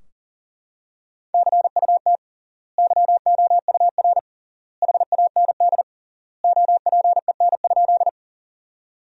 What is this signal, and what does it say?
Morse code, heard as a keyed tone, about 40 wpm, 700 Hz. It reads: CUT YOUR HAND OPEN?
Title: 3x Word-Spacing - Sentences from Top 300 Words - 40wpm